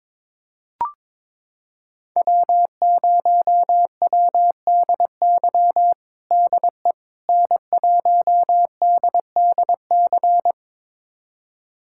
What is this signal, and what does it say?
Morse code, heard as a keyed tone, about 22 wpm, 700 Hz. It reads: W0WDY DE N1DDC
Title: USA Callsign DE Callsign - 5 Characters - 22wpm